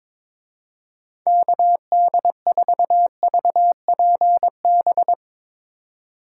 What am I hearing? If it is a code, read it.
Morse code, heard as a keyed tone, about 24 wpm, 700 Hz. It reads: KD4VPB
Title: Speed Racing - USA 2x3 Call Signs 22wpm